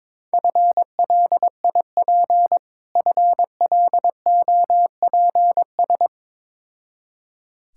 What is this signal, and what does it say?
Morse code, heard as a keyed tone, about 22 wpm, 700 Hz. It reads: FLIP FLOPS
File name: practice / 22wpm / Difficult Words with Hints - 22wpm